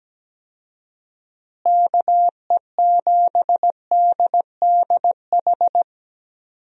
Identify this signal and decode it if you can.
Morse code, heard as a keyed tone, about 17 wpm, 700 Hz. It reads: KE7DDH
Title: USA Call Signs - 6 Characters - Repeated 3x - 17wpm